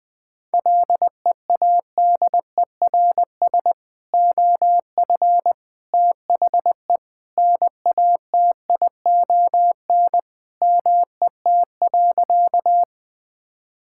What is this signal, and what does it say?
Morse code, heard as a keyed tone, about 20 wpm, 700 Hz. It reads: LEADERS OF THE NATION MET.
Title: Sentences with 3-Word Phrases - 20wpm